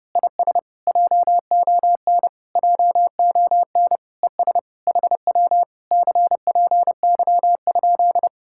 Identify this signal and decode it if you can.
Morse code, heard as a keyed tone, about 29 wpm, 700 Hz. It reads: IS JON JON ES HW CPY?
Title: Speed-Racing Common QSO Phrases - 30wpm